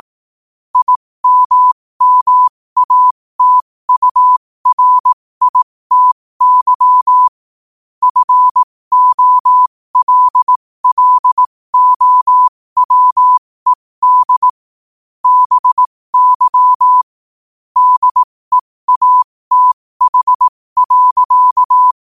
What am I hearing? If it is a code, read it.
Morse code, heard as a keyed tone, about 18 wpm, spaced as for 15 wpm, 1000 Hz. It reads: IMMATURITY FOLLOWED BY DEATH.
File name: 15wpm